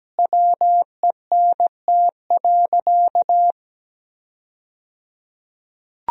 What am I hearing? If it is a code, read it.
Morse code, heard as a keyed tone, about 17 wpm, 700 Hz. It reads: WENT.
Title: 3x Word-Spacing - Sentences from Top 200 Words - 17wpm